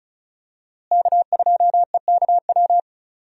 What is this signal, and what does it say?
Morse code, heard as a keyed tone, about 36 wpm, 700 Hz. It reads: K2EKW
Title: USA Call Signs - 5 Characters - Repeated 3x - 35wpm